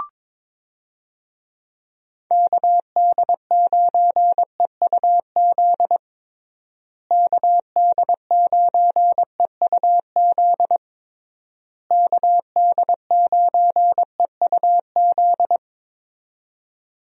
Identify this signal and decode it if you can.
Morse code, heard as a keyed tone, about 22 wpm, 700 Hz. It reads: KD9EUZ KD9EUZ KD9EUZ
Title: USA Call Signs - 6 Characters - Repeated 3x - 22wpm